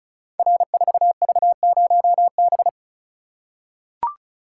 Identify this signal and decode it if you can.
Morse code, heard as a keyed tone, about 36 wpm, 700 Hz. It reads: R4V0B
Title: Code Groups - 5 Letter:Numbers - 35wpm